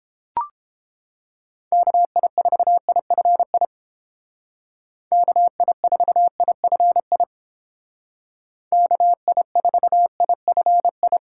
Morse code, heard as a keyed tone, about 28 wpm, 700 Hz. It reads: KI4IFI KI4IFI KI4IFI